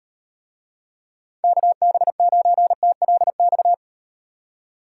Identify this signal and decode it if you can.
Morse code, heard as a keyed tone, about 38 wpm, 700 Hz. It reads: KB9TLX